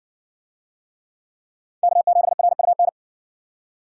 Morse code, heard as a keyed tone, about 53 wpm, 700 Hz. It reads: K7RUN